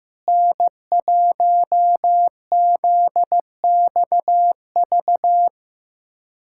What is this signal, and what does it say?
Morse code, heard as a keyed tone, about 16 wpm, 700 Hz. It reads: N1ZXV
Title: USA Callsign DE Callsign - 5 Characters - 15wpm